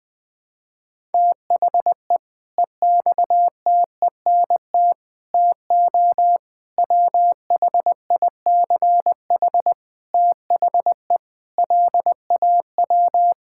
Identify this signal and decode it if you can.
Morse code, heard as a keyed tone, about 21 wpm, 700 Hz. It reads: THE EXTENT TO WHICH THE LAW
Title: Sentences with 5-Word Phrases - 20wpm